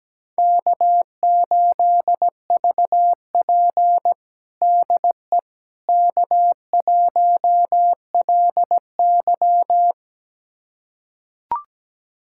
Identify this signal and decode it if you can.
Morse code, heard as a keyed tone, about 17 wpm, 700 Hz. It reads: K8VP DE K1LY
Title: USA Callsign DE Callsign - 4 Characters - 17wpm